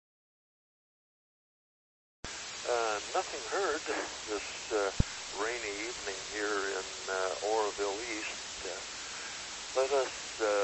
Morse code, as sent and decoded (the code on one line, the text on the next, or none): none